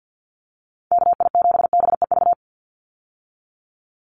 Morse code, heard as a keyed tone, about 50 wpm, 700 Hz. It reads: KI7BEV
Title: USA Call Signs 50wpm